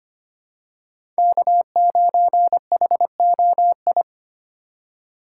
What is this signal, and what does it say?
Morse code, heard as a keyed tone, about 25 wpm, 700 Hz. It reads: K9HOI
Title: USA Call Signs - 4x Character Spacing - 25wpm